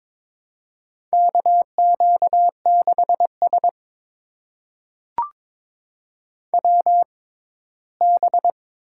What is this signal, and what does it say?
Morse code, heard as a keyed tone, about 22 wpm, spaced as for 15 wpm, 700 Hz. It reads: KQ6S WB
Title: USA Call Signs - 6x Character Spacing - 22wpm